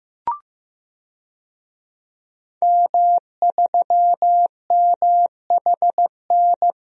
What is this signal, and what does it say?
Morse code, heard as a keyed tone, about 15 wpm, 700 Hz. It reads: M3MHN
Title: DXCC-100 Call Signs - 5 Characters - 15wpm